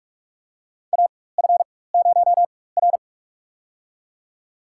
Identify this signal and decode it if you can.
Morse code, heard as a keyed tone, about 44 wpm, spaced as for 21 wpm, 690 Hz. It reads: AF0R
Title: USA Call Signs - 4 Characters - 4x Character Spacing - 45wpm